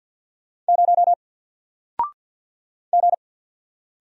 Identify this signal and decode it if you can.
Morse code, heard as a keyed tone, about 48 wpm, 700 Hz. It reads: 0 G
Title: Single Letter-Number Alt 50wpm